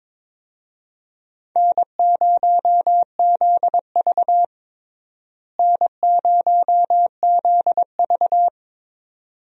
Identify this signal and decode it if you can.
Morse code, heard as a keyed tone, about 22 wpm, 700 Hz. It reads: N0ZV N0ZV
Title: USA Call Signs - 4 Characters - Repeated 2x - 22wpm